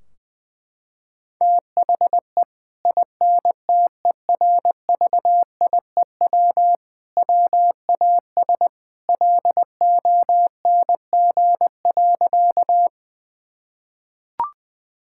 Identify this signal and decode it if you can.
Morse code, heard as a keyed tone, about 20 wpm, 700 Hz. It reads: THE INTERVIEW WAS LONG.
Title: Sentences from Top 1600 Words 20wpm